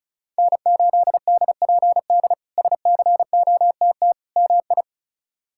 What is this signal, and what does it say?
Morse code, heard as a keyed tone, about 36 wpm, 700 Hz. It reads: N8DPD SCOTT MI
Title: K1USN SST QSO Practice - 35wpm